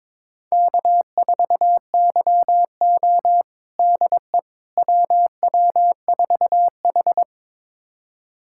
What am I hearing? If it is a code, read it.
Morse code, heard as a keyed tone, about 22 wpm, 700 Hz. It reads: K4YO DE WW4H
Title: USA Callsign DE Callsign - 4 Characters - 22wpm